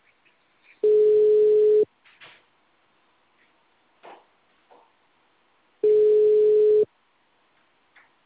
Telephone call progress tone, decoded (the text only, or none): ringback tone